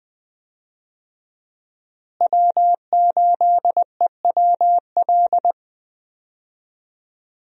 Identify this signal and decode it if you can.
Morse code, heard as a keyed tone, about 20 wpm, 700 Hz. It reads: W8EWL
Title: USA Call Signs - 5 Characters - Repeated 3x - 20wpm